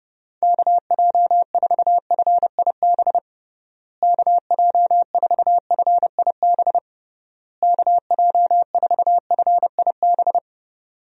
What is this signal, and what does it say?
Morse code, heard as a keyed tone, about 30 wpm, 700 Hz. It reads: KJ4FIB KJ4FIB KJ4FIB
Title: USA Call Signs - 6 Characters - Repeated 3x - 30wpm